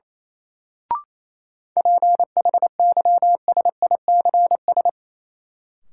Morse code, heard as a keyed tone, about 28 wpm, 700 Hz. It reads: PHYSICS
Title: Sentences from Top 1500 Words - New Word Drill Alt - 28wpm